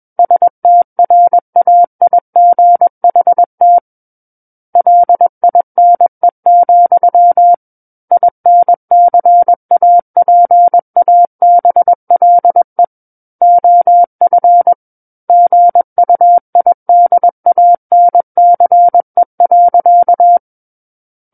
Morse code, heard as a keyed tone, about 21 wpm, 700 Hz. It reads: STRAIGHT LINE, INCAPABLE OF GUIDANCE.
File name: Book One / 12-Chapter 12 - 21 WPM